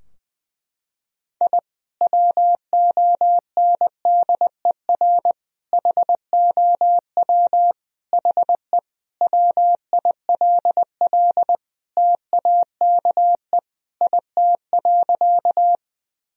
Morse code, heard as a keyed tone, about 21 wpm, 700 Hz. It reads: I WONDER HOW HE WILL TAKE IT.
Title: Sentences from Top 1000 Words - Review Part 3 - 20wpm